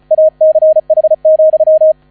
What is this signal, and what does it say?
Morse code, heard as a keyed tone, about 34 wpm, 630 Hz. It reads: ACH,
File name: 35wpm